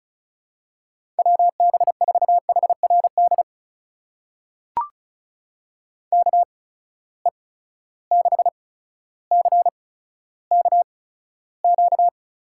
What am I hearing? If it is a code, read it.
Morse code, heard as a keyed tone, about 36 wpm, spaced as for 15 wpm, 700 Hz. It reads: WB4HRD KE6CKQ
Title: USA Call Signs - 6 Characters - 8x Character Spacing - 35wpm